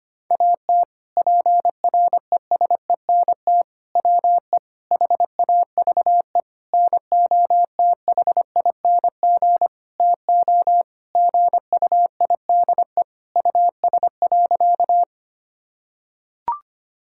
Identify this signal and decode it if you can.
Morse code, heard as a keyed tone, about 25 wpm, 700 Hz. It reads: AT PRESENT WE HAVE NOTHING TO GUIDE US.
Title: Sentences from Top 1000 Words - Review Part 5 - 25wpm